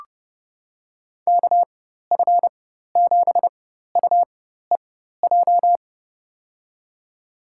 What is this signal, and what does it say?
Morse code, heard as a keyed tone, about 30 wpm, spaced as for 14 wpm, 700 Hz. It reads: KF7UEJ
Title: USA Call Signs - 6 Characters - 4x Character Spacing - 30wpm